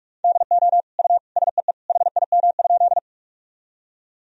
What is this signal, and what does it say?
Morse code, heard as a keyed tone, about 41 wpm, 700 Hz. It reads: DO U SEE HIM?
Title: Sentences from Top 100 Words and Abbreviations 45wpm